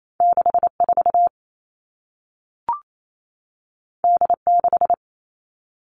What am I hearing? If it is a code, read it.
Morse code, heard as a keyed tone, about 28 wpm, 700 Hz. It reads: 64 D6
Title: All Cut Numbers - 2-Digit 28wpm